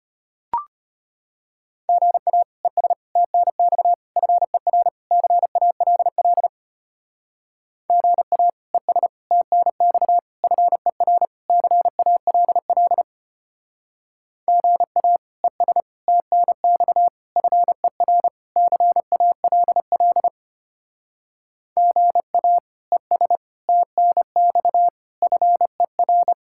Morse code, heard as a keyed tone, about 31 wpm, 700 Hz. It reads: GA ES TNX FER CALL GA ES TNX FER CALL GA ES TNX FER CALL GA ES TNX FER
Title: Speed-Racing Common QSO Phrases Encore - 25wpm